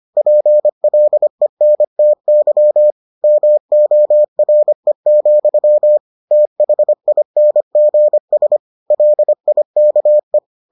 Morse code, heard as a keyed tone, about 25 wpm, 600 Hz. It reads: PLENTY MORE, THINGS LIKE